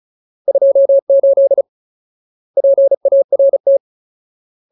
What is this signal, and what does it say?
Morse code, heard as a keyed tone, about 35 wpm, 550 Hz. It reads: 28 PART
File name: Ultimate Word List / 35WPM Ultimate